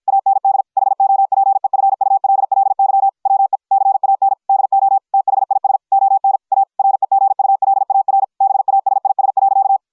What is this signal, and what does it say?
Morse code, heard as a keyed tone, about 53 wpm, 750 Hz. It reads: AND HOPEFULLY WE CAN DO THIS ON A REGULAR BASIS.